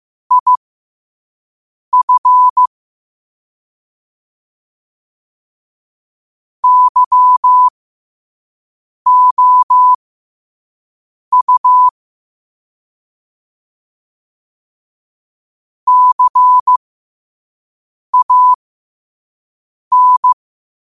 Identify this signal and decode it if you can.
Morse code, heard as a keyed tone, about 15 wpm, spaced as for 5 wpm, 1000 Hz. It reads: IF YOU CAN
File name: qotdfiles / 5wpm